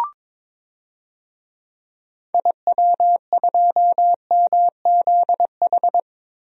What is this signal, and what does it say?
Morse code, heard as a keyed tone, about 22 wpm, 700 Hz. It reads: IW2MZH
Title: DXCC-100 Call Signs - 6 Characters - Repeated 3x - 22wpm